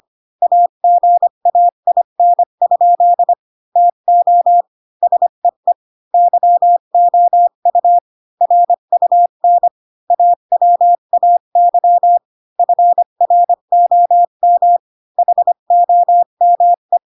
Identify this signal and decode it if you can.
Morse code, heard as a keyed tone, about 25 wpm, 700 Hz. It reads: AGAIN? TO SEE YOU RUN AWAY FROM HOME
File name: Pinocchio / Pinocchio ch8 20 25 ik1hge